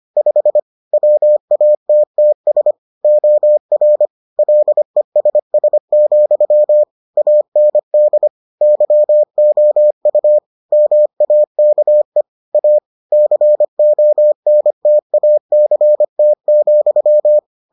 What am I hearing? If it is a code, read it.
Morse code, heard as a keyed tone, about 25 wpm, 600 Hz. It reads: 5 WATTS OR LESS, AND YOU MAKE A CONTACT,